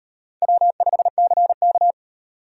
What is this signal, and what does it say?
Morse code, heard as a keyed tone, about 38 wpm, 700 Hz. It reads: W5CK